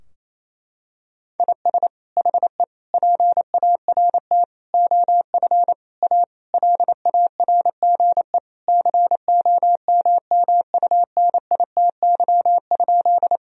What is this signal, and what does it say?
Morse code, heard as a keyed tone, about 28 wpm, 700 Hz. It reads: IS HE PART OF A LARGE COMMUNITY?
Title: Sentences from Top 1000 Words - Review Part 4 - 28wpm